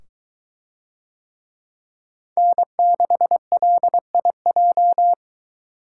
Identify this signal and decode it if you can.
Morse code, heard as a keyed tone, about 22 wpm, 700 Hz. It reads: N6LIJ